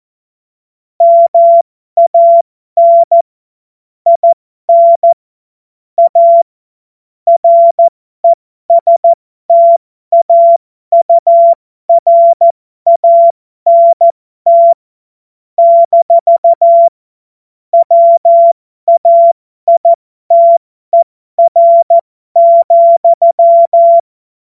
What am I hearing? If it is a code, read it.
Morse code, heard as a keyed tone, about 14 wpm, spaced as for 12 wpm, 680 Hz. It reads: MAN IN A RESTAURANT - WAITER,
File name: Morsekurs / Witze